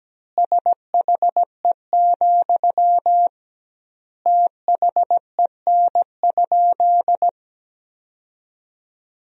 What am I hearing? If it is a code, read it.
Morse code, heard as a keyed tone, about 17 wpm, 700 Hz. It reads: SHE, THEN?
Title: 2x Word-Spacing - Sentences from Top 200 Words - 17wpm